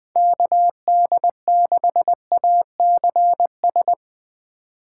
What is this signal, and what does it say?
Morse code, heard as a keyed tone, about 20 wpm, 700 Hz. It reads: KD6ACS
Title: USA Call Signs - 6 Characters - 2x Character Spacing - 20wpm